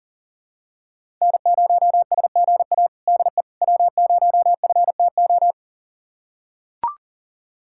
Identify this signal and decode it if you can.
Morse code, heard as a keyed tone, about 40 wpm, 700 Hz. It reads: N0SGA DE W0FTO
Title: USA Callsign DE Callsign - 5 Characters - 40wpm